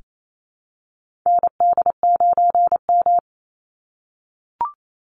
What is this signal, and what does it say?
Morse code, heard as a keyed tone, about 28 wpm, 700 Hz. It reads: ND9M
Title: Call Signs - 28wpm